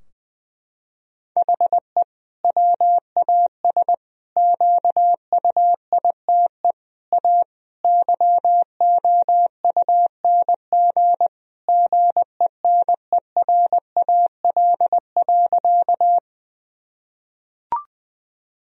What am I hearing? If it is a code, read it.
Morse code, heard as a keyed tone, about 20 wpm, 700 Hz. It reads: HE WAS QUITE A YOUNG GENERAL.
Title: Sentences from Top 1000 Words - Review Part 3 - 20wpm